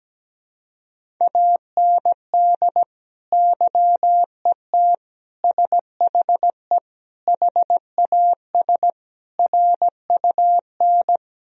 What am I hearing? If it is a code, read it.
Morse code, heard as a keyed tone, about 17 wpm, 700 Hz. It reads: AND YET SHE HAS RUN
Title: Sentences from Top 1000 Words - Review Part 1 - 17wpm